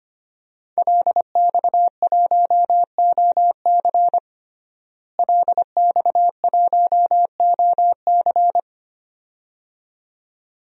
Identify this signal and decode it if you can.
Morse code, heard as a keyed tone, about 25 wpm, 700 Hz. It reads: LX1OC LX1OC
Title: DXCC-100 Call Signs - 5 Characters - Repeated 2x - 25wpm